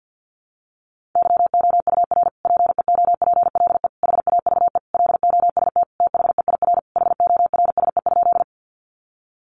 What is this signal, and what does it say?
Morse code, heard as a keyed tone, about 50 wpm, 700 Hz. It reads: YOUR PEOPLE HAVE LOST THEIR HORSE?
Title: Sentences from Top 1400 Words 50wpm